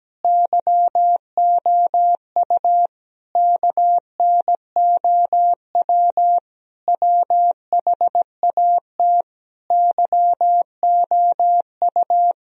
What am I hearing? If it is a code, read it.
Morse code, heard as a keyed tone, about 17 wpm, 700 Hz. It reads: YOU KNOW WHAT YOU
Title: Sentences from Top 1000 Words - Review Part 3 - 17wpm